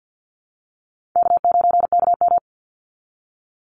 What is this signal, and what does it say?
Morse code, heard as a keyed tone, about 50 wpm, 700 Hz. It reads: K9KM
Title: Easy Call Signs Repeated 2x - 50wpm